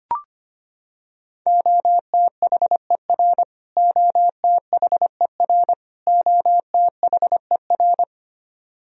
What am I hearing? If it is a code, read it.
Morse code, heard as a keyed tone, about 25 wpm, 700 Hz. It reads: OTHER OTHER OTHER
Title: Top 100 Words Repeated 25wpm